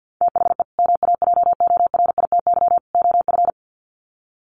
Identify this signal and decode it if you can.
Morse code, heard as a keyed tone, about 50 wpm, 700 Hz. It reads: THE MAJORITY OF
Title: Sets of 3 Words 50wpm